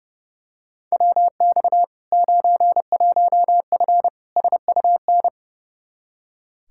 Morse code, heard as a keyed tone, about 32 wpm, 700 Hz. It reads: WX 91F SUN
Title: Practice QSO - Exchange 2 - Info - 30wpm